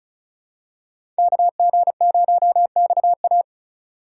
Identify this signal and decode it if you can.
Morse code, heard as a keyed tone, about 34 wpm, 700 Hz. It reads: KG0XA
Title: USA Call Signs - 6x Character Spacing - 35wpm